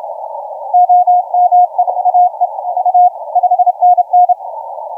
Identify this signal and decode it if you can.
Morse code, heard as a keyed tone, about 28 wpm, 720 Hz, with radio noise. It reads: OME AE A 5NN
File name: 3B9EME 1955UTC 40m